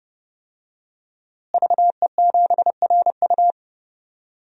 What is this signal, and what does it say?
Morse code, heard as a keyed tone, about 32 wpm, 700 Hz. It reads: VE7RU
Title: DXCC-100 Call Signs - 5 Characters - Repeated 2x - 30wpm